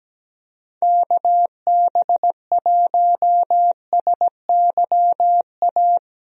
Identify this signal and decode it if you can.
Morse code, heard as a keyed tone, about 17 wpm, 700 Hz. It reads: KB1SYA
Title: USA Call Signs - 6 Characters - Repeated 2x - 17wpm